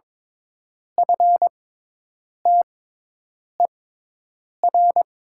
Morse code, heard as a keyed tone, about 22 wpm, spaced as for 8 wpm, 700 Hz. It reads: FTER
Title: 6x Character Spacing - Top 200 Words - 22wpm